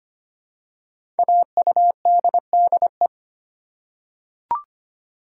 Morse code, heard as a keyed tone, about 25 wpm, 700 Hz. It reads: AUDDE
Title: Code Groups - 5 Letters - 25wpm